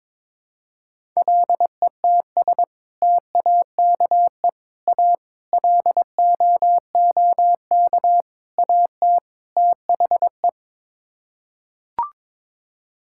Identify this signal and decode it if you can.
Morse code, heard as a keyed tone, about 23 wpm, 700 Hz. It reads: LETS TAKE A LOOK AT THE
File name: practice / 22wpm / Sets of 6 Words - 22wpm